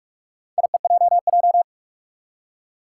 Morse code, heard as a keyed tone, about 41 wpm, 700 Hz. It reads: IEJJ
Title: Four Challenging Letter-Number Combinations 45wpm